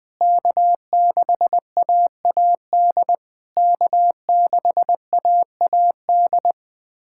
Morse code, heard as a keyed tone, about 20 wpm, 700 Hz. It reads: K6AAD K6AAD
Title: SOTA QSO Practice 20wpm